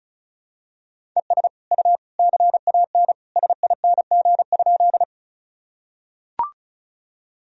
Morse code, heard as a keyed tone, about 36 wpm, 700 Hz. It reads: ES U CAN SING?
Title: Sentences from Top 400 Words and Abbreviations 35wpm